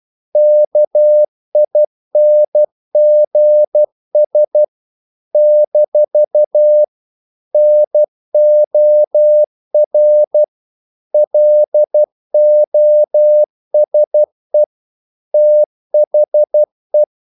Morse code, heard as a keyed tone, about 12 wpm, 600 Hz. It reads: KINGS - NOR LOSE THE
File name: CW / Practice / If12